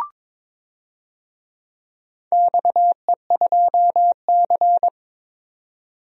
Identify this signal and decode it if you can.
Morse code, heard as a keyed tone, about 22 wpm, 700 Hz. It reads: XE2C
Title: DXCC-100 Call Signs - 4 Characters - 22wpm